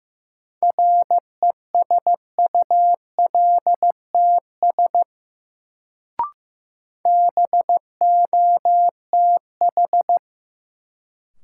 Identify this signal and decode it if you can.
Morse code, heard as a keyed tone, about 15 wpm, 700 Hz. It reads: RESULTS BOTH
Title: Sets of 4 Words - New Word Drill - 15wpm